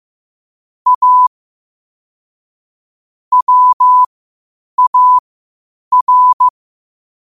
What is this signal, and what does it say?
Morse code, heard as a keyed tone, about 15 wpm, spaced as for 8 wpm, 1000 Hz. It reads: A WAR